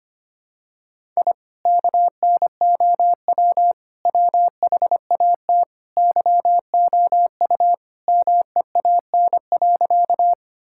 Morse code, heard as a keyed tone, about 25 wpm, 700 Hz. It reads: I KNOW WHAT YOU MEAN.
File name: practice / 25wpm / Sentences with 2-Word Phrases - 25wpm